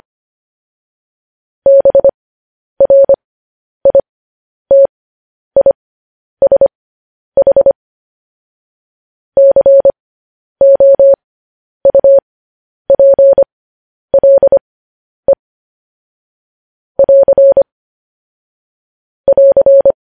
Morse code, heard as a keyed tone, about 25 wpm, spaced as for 10 wpm, 550 Hz. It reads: BRITISH COUPLE <AR> <AR>
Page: news headlines in morse code at 10 wpm